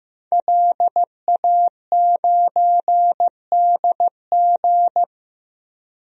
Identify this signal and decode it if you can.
Morse code, heard as a keyed tone, about 15 wpm, 700 Hz. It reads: LA9DG
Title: DXCC-100 Call Signs - 5 Characters - 15wpm